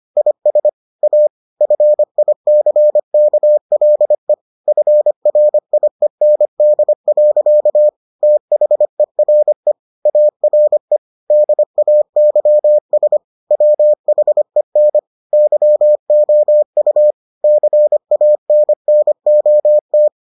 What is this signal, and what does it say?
Morse code, heard as a keyed tone, about 25 wpm, 600 Hz. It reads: IS A FICKLE FRIEND. THERE ARE DAYS WHEN YOU CANNOT